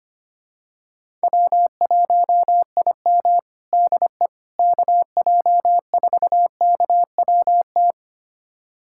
Morse code, heard as a keyed tone, about 25 wpm, 700 Hz. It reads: W1IM DE KJ4KWT
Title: USA Callsign DE Callsign 25wpm